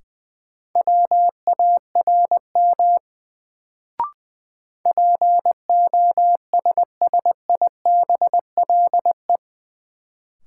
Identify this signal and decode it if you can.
Morse code, heard as a keyed tone, about 21 wpm, 700 Hz. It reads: WARM POSSIBLE
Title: Sentences from Top 500 Words - New Word Drill - 20wpm